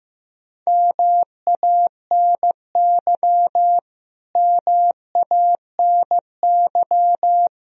Morse code, heard as a keyed tone, about 15 wpm, 700 Hz. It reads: MANY MANY
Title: Top 100 Words Repeated 15wpm